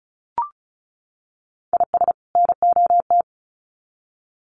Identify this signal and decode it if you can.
Morse code, heard as a keyed tone, about 34 wpm, 700 Hz. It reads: IS NOT
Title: Sets of 2 Words 35wpm